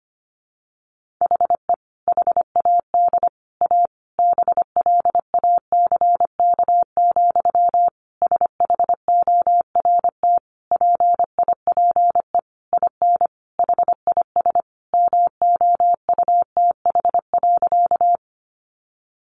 Morse code, heard as a keyed tone, about 25 wpm, 700 Hz. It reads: HE HAD A BLACK, SHORT PIPE IN HIS MOUTH.
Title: Sentences from Top 1500 Words 25wpm